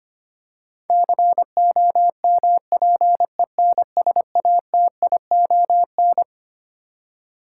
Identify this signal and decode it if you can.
Morse code, heard as a keyed tone, about 25 wpm, 700 Hz. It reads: COMPENSATION
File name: Twelve Letter Words - 25wpm